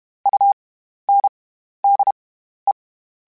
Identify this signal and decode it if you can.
Morse code, heard as a keyed tone, about 32 wpm, spaced as for 13 wpm, 800 Hz. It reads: UNDE
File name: Special INGs / Special INGs-mp3s / SlowToFaster ing @30